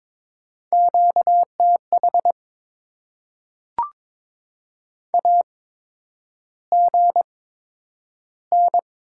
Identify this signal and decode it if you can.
Morse code, heard as a keyed tone, about 22 wpm, spaced as for 9 wpm, 700 Hz. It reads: QTH AGN
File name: Top 20 QSO Elements - 8x Character Spacing - 22wpm